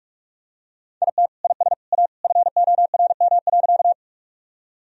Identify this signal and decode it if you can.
Morse code, heard as a keyed tone, about 46 wpm, 700 Hz. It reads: IT IS A FORM.
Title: Sentences from Top 1000 Words - Review Part 1 - 45wpm